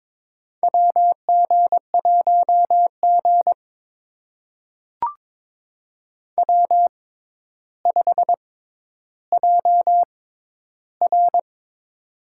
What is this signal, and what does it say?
Morse code, heard as a keyed tone, about 22 wpm, spaced as for 11 wpm, 700 Hz. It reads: WG1G W5JR
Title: USA Call Signs - 4 Characters - 6x Character Spacing - 22wpm